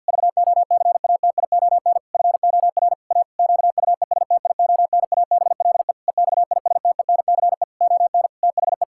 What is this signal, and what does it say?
Morse code, heard as a keyed tone, about 51 wpm, 700 Hz. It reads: VOCATION FOR A QUESTIONABLE EXISTENCE ON THE
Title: Pangram Challenge v2 - 50wpm